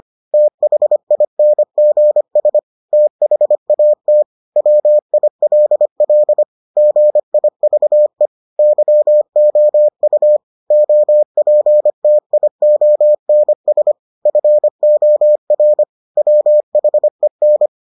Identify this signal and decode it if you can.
Morse code, heard as a keyed tone, about 25 wpm, 600 Hz. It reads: THINGS THAT WILL GIVE YOU OPTIONS FOR WHEN